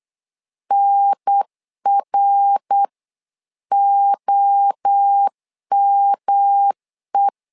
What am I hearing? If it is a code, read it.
Morse code, heard as a keyed tone, about 8 wpm, 800 Hz. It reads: NROME
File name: LEZIONI / 4 Q1S3 8wpm 3037